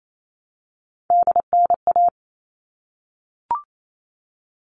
Morse code, heard as a keyed tone, about 28 wpm, 700 Hz. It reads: DNA
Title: All Cut Numbers - 3-Digit 28wpm